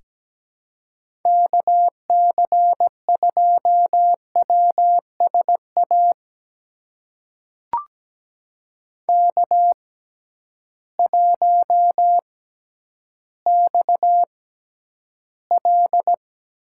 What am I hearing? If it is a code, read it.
Morse code, heard as a keyed tone, about 17 wpm, spaced as for 10 wpm, 700 Hz. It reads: KC2WSA K1XL
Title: USA Call Signs - 6x Character Spacing - 17wpm